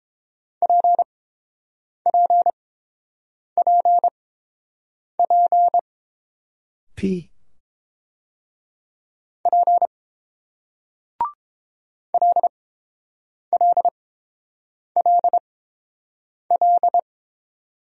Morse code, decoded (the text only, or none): PPPP P LLLL